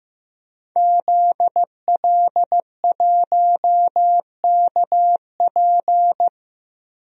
Morse code, heard as a keyed tone, about 15 wpm, 700 Hz. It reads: ZL1KP
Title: DXCC-100 Call Signs - 5 Characters - Repeated 3x - 15wpm